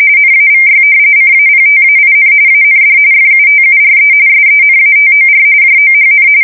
RTTY, about 45.45 baud. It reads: NAVY CAPTAIN AND FUTURE SENATOR JOHN
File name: fsk170 / battleshipnj